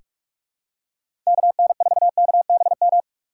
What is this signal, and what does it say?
Morse code, heard as a keyed tone, about 46 wpm, 700 Hz. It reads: KN4KBM
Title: USA Call Signs - 6 Characters - 3x Character Spacing - 45wpm